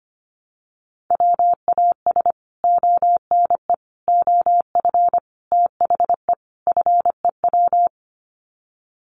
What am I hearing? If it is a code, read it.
Morse code, heard as a keyed tone, about 25 wpm, 700 Hz. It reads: WAS ONE OF THE FEW
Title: Sets of 5 Words Encore 25wpm